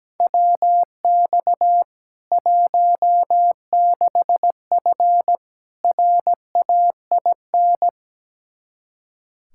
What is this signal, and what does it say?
Morse code, heard as a keyed tone, about 17 wpm, 700 Hz. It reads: WX 16F RAIN
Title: K1USN SST QSO Practice 17wpm